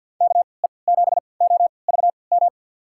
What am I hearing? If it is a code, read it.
Morse code, heard as a keyed tone, about 48 wpm, spaced as for 28 wpm, 700 Hz. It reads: KE7OVM
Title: USA Call Signs - 6 Characters - 3x Character Spacing - 50wpm